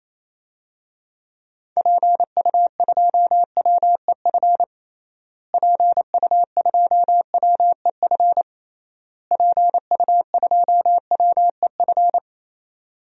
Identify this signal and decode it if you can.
Morse code, heard as a keyed tone, about 28 wpm, 700 Hz. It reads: PU2WEF PU2WEF PU2WEF
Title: DXCC-100 Call Signs - 6 Characters - Repeated 3x - 28wpm